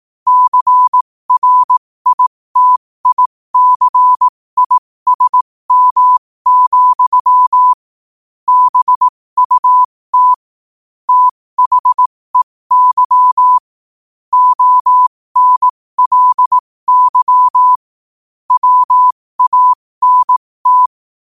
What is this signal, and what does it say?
Morse code, heard as a keyed tone, about 18 wpm, spaced as for 15 wpm, 1000 Hz. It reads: CRITICISM, BUT THEY ONLY WANT
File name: qotdfiles / 15wpm